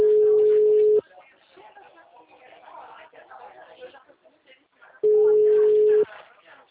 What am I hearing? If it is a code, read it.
Ringback tone (Europe, 425 Hz), with speech.
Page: TLMK - Rio de janeiro 2